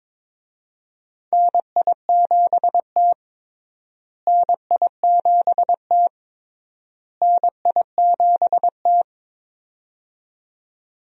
Morse code, heard as a keyed tone, about 22 wpm, 700 Hz. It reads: NI7T NI7T NI7T
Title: USA Call Signs - 4 Characters - Repeated 3x - 22wpm